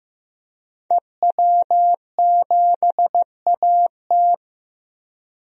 Morse code, heard as a keyed tone, about 15 wpm, 700 Hz. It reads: EW7AT
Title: DXCC-100 Call Signs - 5 Characters - Repeated 3x - 15wpm